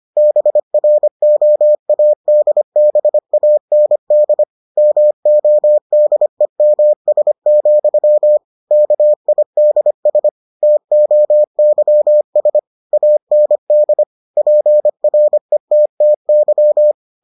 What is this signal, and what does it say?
Morse code, heard as a keyed tone, about 25 wpm, 600 Hz. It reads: BROADBAND MODEMS, KIDS TOYS AND PRETTY